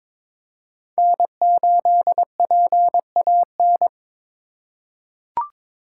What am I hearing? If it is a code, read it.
Morse code, heard as a keyed tone, about 22 wpm, 700 Hz. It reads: N8PAN
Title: USA Call Signs - 5 Characters - 8x Character Spacing - 22wpm